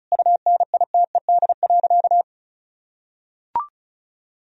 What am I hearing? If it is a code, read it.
Morse code, heard as a keyed tone, about 34 wpm, 700 Hz. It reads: UNITED.
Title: Sentences from Top 1000 Words - Review Part 4 - 35wpm